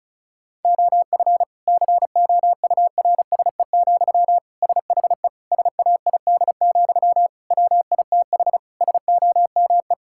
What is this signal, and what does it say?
Morse code, heard as a keyed tone, about 36 wpm, 700 Hz. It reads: OF COURSE, SHE SAID, WITH SOME
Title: Sentences from Top 1600 Words 35wpm